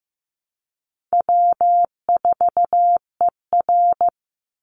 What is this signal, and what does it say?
Morse code, heard as a keyed tone, about 15 wpm, 700 Hz. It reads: W4ER
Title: Easy Call Signs - 15wpm